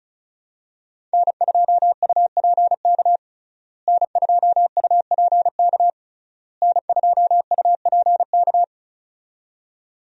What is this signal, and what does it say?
Morse code, heard as a keyed tone, about 36 wpm, 700 Hz. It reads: N2UPK N2UPK N2UPK
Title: USA Call Signs - 5 Characters - Repeated 3x - 35wpm